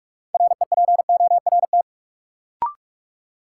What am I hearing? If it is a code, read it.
Morse code, heard as a keyed tone, about 46 wpm, 700 Hz. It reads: REPORT
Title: Sentences from Top 900 Words - New Word Drill - 45wpm